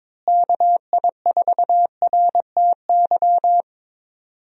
Morse code, heard as a keyed tone, about 22 wpm, 700 Hz. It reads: KI4RTY
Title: USA Call Signs - 6 Characters - Repeated 2x - 22wpm